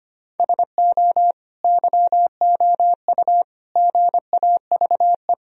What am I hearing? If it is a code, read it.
Morse code, heard as a keyed tone, about 25 wpm, 700 Hz. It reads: SO YOU GAVE
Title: Sentences from Top 1000 Words - Review Part 2 - 25wpm